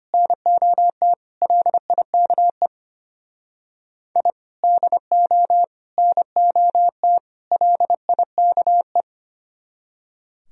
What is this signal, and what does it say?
Morse code, heard as a keyed tone, about 32 wpm, 700 Hz. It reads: NOT LIKE I DO NOT LIKE